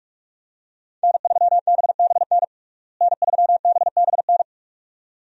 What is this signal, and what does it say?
Morse code, heard as a keyed tone, about 46 wpm, 700 Hz. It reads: N3BBN N3BBN
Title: USA Call Signs - 5 Characters - Repeated 2x - 45wpm